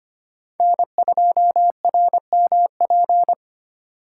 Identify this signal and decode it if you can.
Morse code, heard as a keyed tone, about 25 wpm, 700 Hz. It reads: N2RMP
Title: USA Call Signs - 5 Characters - 4x Character Spacing - 25wpm